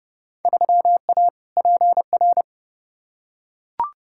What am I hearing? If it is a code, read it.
Morse code, heard as a keyed tone, about 30 wpm, 700 Hz. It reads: 3A PR
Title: Field Day Exchanges Encore - 30wpm